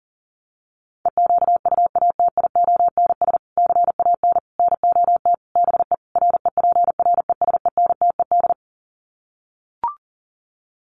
Morse code, heard as a keyed tone, about 37 wpm, 700 Hz. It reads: EQUATIONS CAN NOT BE REPRESENTED
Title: Sets of 5 Words 40wpm